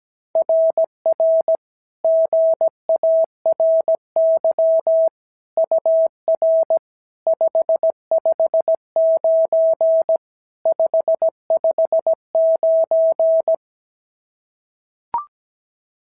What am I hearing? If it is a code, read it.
Morse code, heard as a keyed tone, about 16 wpm, 650 Hz. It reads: RR GARY UR 559 559
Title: SOTA QSO Practice 17wpm